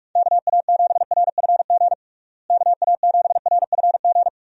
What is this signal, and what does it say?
Morse code, heard as a keyed tone, about 43 wpm, 700 Hz. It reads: KA7RFG KA7RFG
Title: USA Call Signs - 6 Characters - Repeated 2x - 45wpm